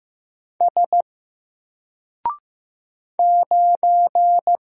Morse code, heard as a keyed tone, about 15 wpm, 700 Hz. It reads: S 9
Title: Single Character (Letter:Number:Punctuation:Prosign) 15wpm